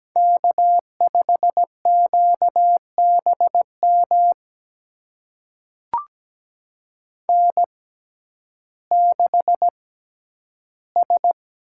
Morse code, heard as a keyed tone, about 17 wpm, spaced as for 10 wpm, 700 Hz. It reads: K5QBM N6S
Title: USA Call Signs - 5 Characters - 6x Character Spacing - 17wpm